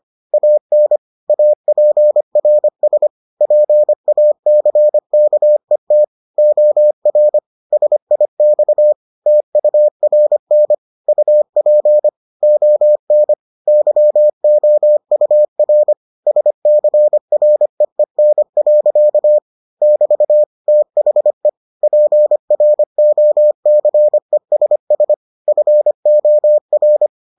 Morse code, heard as a keyed tone, about 25 wpm, 600 Hz. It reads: AN APRS PACKET OR SIX TURN UP ON YOUR SCREEN. <BT> THE PROCESS FOR